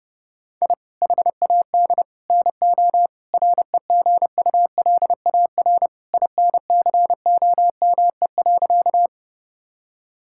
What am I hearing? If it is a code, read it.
Morse code, heard as a keyed tone, about 32 wpm, 700 Hz. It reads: I HAD NO REGULAR INCOME.
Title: Sentences from Top 1000 Words - Review Part 6 - 30wpm